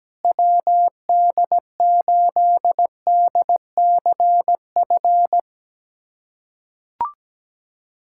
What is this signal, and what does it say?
Morse code, heard as a keyed tone, about 17 wpm, 700 Hz. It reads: WD8DCF
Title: USA Callsign DE Callsign - 6 Characters - 17wpm